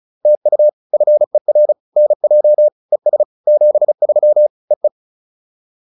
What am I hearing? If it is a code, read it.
Morse code, heard as a keyed tone, about 34 wpm, 600 Hz. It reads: TU FER NJ ES 73 EE